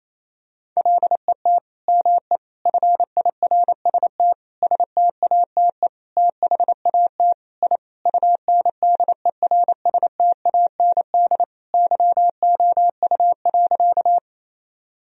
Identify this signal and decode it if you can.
Morse code, heard as a keyed tone, about 28 wpm, 700 Hz. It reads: LET ME FIRST STATE THAT I UNDERSTAND YOU.
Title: Sentences from Top 1000 Words - Review Part 2 - 28wpm